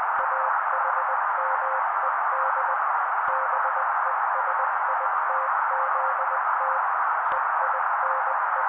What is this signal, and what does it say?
Morse code, heard as a keyed tone, about 20 wpm, 530 Hz, with noise. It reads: AHMED BESITZT EINE
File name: daily / 18wpm noise